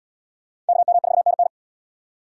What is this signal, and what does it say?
Morse code, heard as a keyed tone, about 77 wpm, 700 Hz. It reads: KM4NN